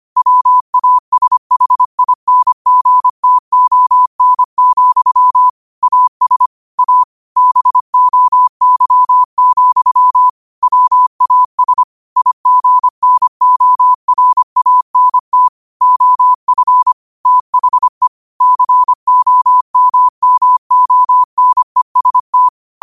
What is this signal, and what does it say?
Morse code, heard as a keyed tone, about 25 wpm, 1000 Hz. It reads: WASHINGTON, AS A BOY, WAS IGNORANT OF THE COMMONEST